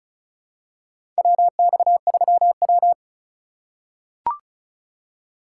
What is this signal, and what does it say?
Morse code, heard as a keyed tone, about 36 wpm, 700 Hz. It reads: WX3W
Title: USA Call Signs - 4 Characters - 3x Character Spacing - 35wpm